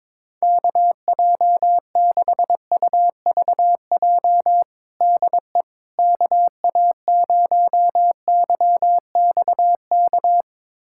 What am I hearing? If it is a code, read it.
Morse code, heard as a keyed tone, about 22 wpm, 700 Hz. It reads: KJ6UVJ DE KA0YXK
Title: USA Callsign DE Callsign - 6 Characters - 22wpm